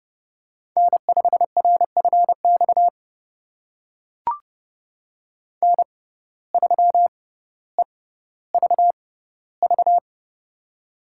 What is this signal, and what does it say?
Morse code, heard as a keyed tone, about 30 wpm, spaced as for 15 wpm, 700 Hz. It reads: N5RFX N3EVV
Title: USA Call Signs - 5 Characters - 6x Character Spacing - 30wpm